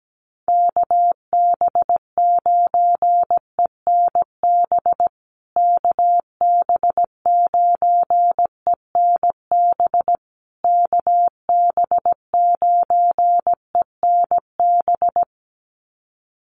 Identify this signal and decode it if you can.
Morse code, heard as a keyed tone, about 17 wpm, 700 Hz. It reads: KB9ENB KB9ENB KB9ENB
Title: USA Call Signs Repeated 3x 17wpm